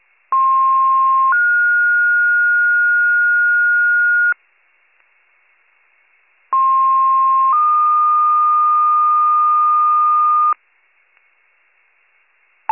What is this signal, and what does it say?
Two-tone page, heard as a one-second tone, then a three-second tone, about 1060 then 1430 Hz, about 1060 then 1220 Hz. It is heard from 0.3 to 10.5 s.